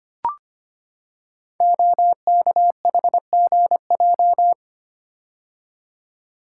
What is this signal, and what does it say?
Morse code, heard as a keyed tone, about 25 wpm, 700 Hz. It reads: OKHGJ
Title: Code Groups - 5 Letter:Numbers - 25wpm